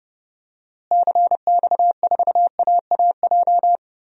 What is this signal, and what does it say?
Morse code, heard as a keyed tone, about 31 wpm, 700 Hz. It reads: CX4AAJ
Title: DXCC-100 Call Signs - 6 Characters - Repeated 3x - 30wpm